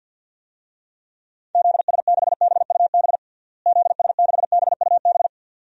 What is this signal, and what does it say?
Morse code, heard as a keyed tone, about 52 wpm, 700 Hz. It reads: ZS6BUB ZS6BUB
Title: DXCC-100 Call Signs - 6 Characters - Repeated 2x - 50wpm